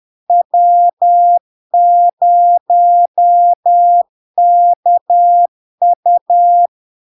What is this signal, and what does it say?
Morse code, heard as a keyed tone, about 10 wpm, 700 Hz. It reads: W0KU
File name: LARK-FD-2022-Log-in-CW-10wpm-700-Hz